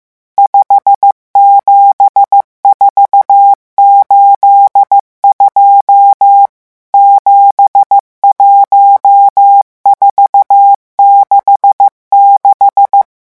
Morse code, heard as a keyed tone, about 15 wpm, 800 Hz. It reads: 57482 71466
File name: morse mp3 / revision11 chiffres vitesse 14